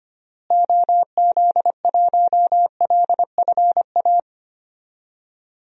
Morse code, heard as a keyed tone, about 25 wpm, 700 Hz. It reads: OZ1LFA
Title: DXCC-100 Call Signs - 6 Characters - Repeated 3x - 25wpm